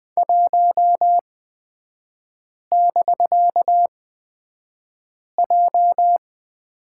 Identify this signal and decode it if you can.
Morse code, heard as a keyed tone, about 20 wpm, spaced as for 5 wpm, 700 Hz. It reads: 1<BK>J
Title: 250 Lesson 40e - BK - Three Characters 20wpm